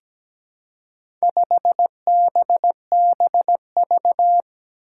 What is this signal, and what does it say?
Morse code, heard as a keyed tone, about 17 wpm, 700 Hz. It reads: 5BBV
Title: Four Challenging Letter-Number Combinations 17wpm